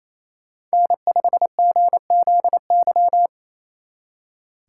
Morse code, heard as a keyed tone, about 33 wpm, 700 Hz. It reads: N5GZY
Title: Speed Racing - USA 1x3 Call Signs 28wpm